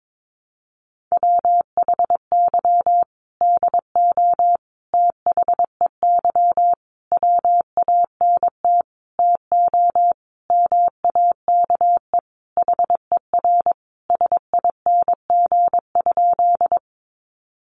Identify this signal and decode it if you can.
Morse code, heard as a keyed tone, about 22 wpm, 700 Hz. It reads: WHY DO THEY WANT TO MAKE HER SING?
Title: Sentences from Top 400 Words 22wpm